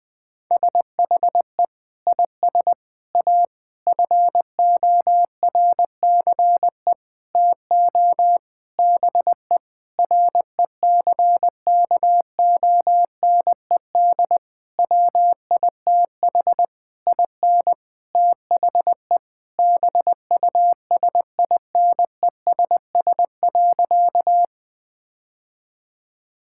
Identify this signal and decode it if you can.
Morse code, heard as a keyed tone, about 20 wpm, 700 Hz. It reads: SHE IS A FORCE TO BE RECKONED WITH IN THE BUSINESS.
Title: Sentences with 8-Word Phrases - 20wpm